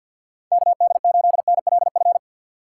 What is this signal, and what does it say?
Morse code, heard as a keyed tone, about 65 wpm, spaced as for 57 wpm, 700 Hz. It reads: KD8NLF
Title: Speed Racing - USA 2x3 Call Signs 50wpm